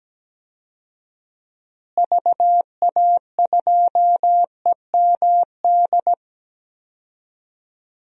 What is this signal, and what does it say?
Morse code, heard as a keyed tone, about 17 wpm, 700 Hz. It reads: VA2EMD
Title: DXCC-100 Call Signs - 6 Characters - Repeated 3x - 17wpm